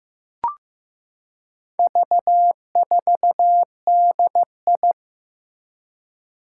Morse code, heard as a keyed tone, about 15 wpm, 700 Hz. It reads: V4DI
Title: Four Challenging Letter-Number Combinations 15wpm